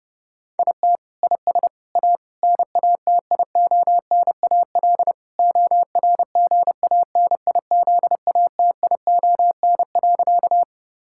Morse code, heard as a keyed tone, about 30 wpm, 700 Hz. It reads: IT IS A NATIONAL ORGANIZATION.
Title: Sentences from Top 1000 Words - Review Part 5 - 30wpm